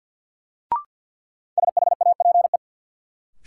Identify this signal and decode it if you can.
Morse code, heard as a keyed tone, about 53 wpm, 700 Hz. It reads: SHAPE
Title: Sets of 5 Words Encore - New Word Drill - 50wpm